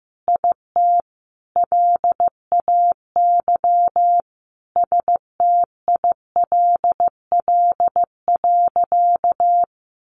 Morse code, heard as a keyed tone, about 15 wpm, 700 Hz. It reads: IT LAY STILL.
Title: Sentences from Top 1000 Words 15wpm